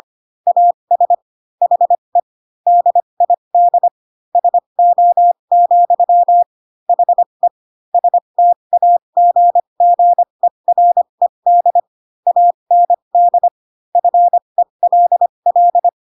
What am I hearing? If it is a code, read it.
Morse code, heard as a keyed tone, about 25 wpm, spaced as for 22 wpm, 700 Hz. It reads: AS HE DID SO, HE STAGGERED AND FELL
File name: Pinocchio / Pinocchio ch7 18 25 ik1hge